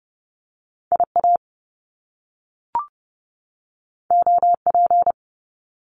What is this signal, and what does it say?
Morse code, heard as a keyed tone, about 30 wpm, 700 Hz. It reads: IA OP
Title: Top Two Letter Combinations in Words 30wpm